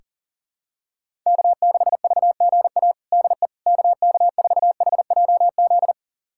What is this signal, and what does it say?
Morse code, heard as a keyed tone, about 40 wpm, 700 Hz. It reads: K6VGA DE KK4HJZ
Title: USA Callsign DE Callsign 40wpm